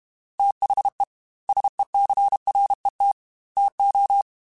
Morse code, heard as a keyed tone, about 32 wpm, 800 Hz. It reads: THE SECRET TO
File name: Technical Subjects / Technical Subjects-mp3s / Sw Pwr Sup @30